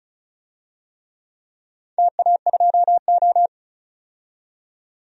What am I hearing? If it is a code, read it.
Morse code, heard as a keyed tone, about 36 wpm, 700 Hz. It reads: TA2O
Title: DXCC-100 Call Signs - 4 Characters - 35wpm